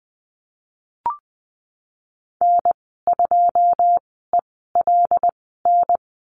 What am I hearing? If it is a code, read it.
Morse code, heard as a keyed tone, about 20 wpm, spaced as for 15 wpm, 700 Hz. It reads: N2ELN
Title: 088 Lesson 15g - Number 5 - Call Signs 20wpm